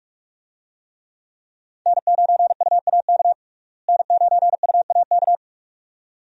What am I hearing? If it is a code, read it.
Morse code, heard as a keyed tone, about 46 wpm, 700 Hz. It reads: N9UAK N9UAK
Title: USA Call Signs - 5 Characters - Repeated 2x - 45wpm